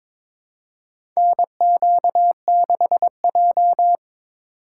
Morse code, heard as a keyed tone, about 22 wpm, 700 Hz. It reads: NQ6J
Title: USA Call Signs - 4 Characters - 3x Character Spacing - 22wpm